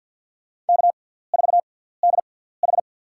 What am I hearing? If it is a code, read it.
Morse code, heard as a keyed tone, about 46 wpm, spaced as for 17 wpm, 700 Hz. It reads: K4DH
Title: USA Call Signs - 4 Characters - 6x Character Spacing - 50wpm